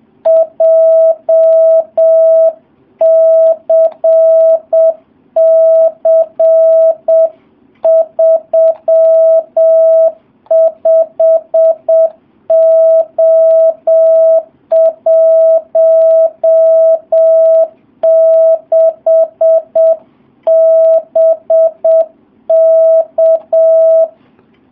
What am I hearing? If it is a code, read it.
Morse code, heard as a keyed tone, about 7 wpm, 650 Hz. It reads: JCC35O16 BK
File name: jh8cbh / 2019cw